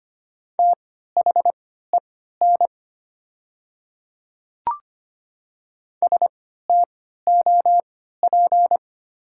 Morse code, heard as a keyed tone, about 25 wpm, spaced as for 14 wpm, 690 Hz. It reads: THEN STOP
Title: 3x Character Spacing - Top 500 Words - 25wpm